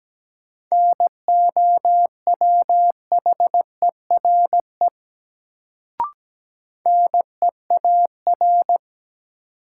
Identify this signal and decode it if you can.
Morse code, heard as a keyed tone, about 17 wpm, 700 Hz. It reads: NOWHERE NEAR
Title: Sets of 5 Words Encore - New Word Drill - 17wpm